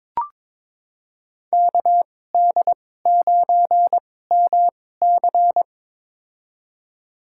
Morse code, heard as a keyed tone, about 22 wpm, spaced as for 16 wpm, 700 Hz. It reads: KD9MC
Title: USA Call Signs - 5 Characters - 2x Character Spacing - 22wpm